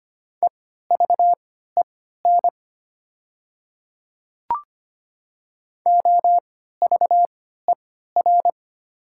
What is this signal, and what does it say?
Morse code, heard as a keyed tone, about 25 wpm, spaced as for 14 wpm, 690 Hz. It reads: EVEN OVER
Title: 3x Character Spacing - Top 500 Words - 25wpm